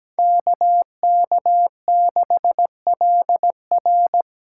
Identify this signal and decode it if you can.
Morse code, heard as a keyed tone, about 17 wpm, 700 Hz. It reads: KK6LR
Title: USA Call Signs - 8x Character Spacing - 17wpm